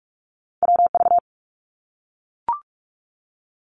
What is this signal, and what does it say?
Morse code, heard as a keyed tone, about 44 wpm, 700 Hz. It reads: WV
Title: US States Repeated 3x 45wpm